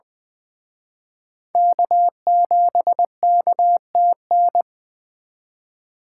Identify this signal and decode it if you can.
Morse code, heard as a keyed tone, about 23 wpm, 700 Hz. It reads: K7KTN
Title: Speed Racing - USA Call Signs 20wpm